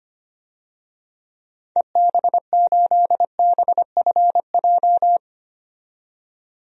Morse code, heard as a keyed tone, about 25 wpm, 700 Hz. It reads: EB8BFJ
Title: DXCC-100 Call Signs - 6 Characters - 25wpm